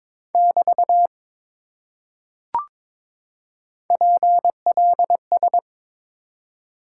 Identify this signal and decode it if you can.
Morse code, heard as a keyed tone, about 22 wpm, 700 Hz. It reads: <BT> PLS
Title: Top 90 QSO Elements 22wpm